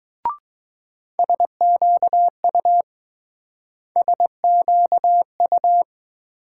Morse code, heard as a keyed tone, about 20 wpm, 700 Hz. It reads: SQU SQU